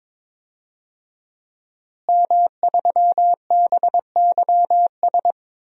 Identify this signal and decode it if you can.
Morse code, heard as a keyed tone, about 22 wpm, 700 Hz. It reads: M3BYS
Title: DXCC-100 Call Signs - 5 Characters - Repeated 2x - 22wpm